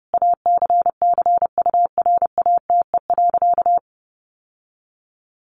Morse code, heard as a keyed tone, about 28 wpm, 700 Hz. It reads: ACCURATE.
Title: Sentences from Top 1200 Words 30wpm